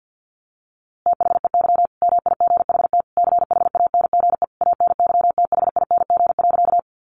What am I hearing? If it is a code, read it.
Morse code, heard as a keyed tone, about 50 wpm, 700 Hz. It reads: THEY MIGHT CHANGE ANYTHING.
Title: A Sound Of Thunder 50wpm